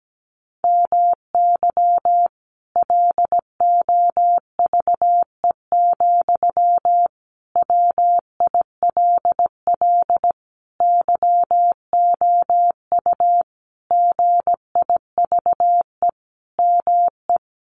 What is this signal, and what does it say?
Morse code, heard as a keyed tone, about 17 wpm, 700 Hz. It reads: MY LOVE, WILL YOU GIVE ME
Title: Sentences from Top 1200 Words 17wpm